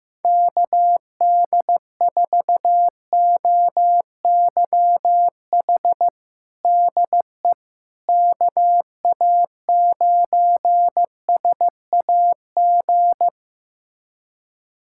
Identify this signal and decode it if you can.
Morse code, heard as a keyed tone, about 15 wpm, 700 Hz. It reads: KD4OYH DE KA9SAG
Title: USA Callsign DE Callsign - 6 Characters - 15wpm